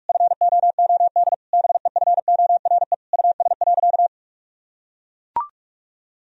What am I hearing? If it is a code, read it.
Morse code, heard as a keyed tone, about 41 wpm, 700 Hz. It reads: FOOD BEFORE US.
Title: Sentences from Top 1000 Words - Review Part 4 - 45wpm